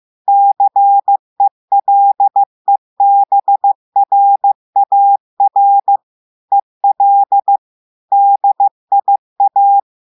Morse code, heard as a keyed tone, about 15 wpm, 800 Hz. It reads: CELEBRAR EL DIA
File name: eh1ard-amateur-radio-day-2016 mf feed 1